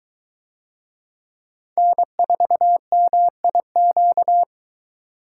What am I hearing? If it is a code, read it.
Morse code, heard as a keyed tone, about 22 wpm, 700 Hz. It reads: N4MIQ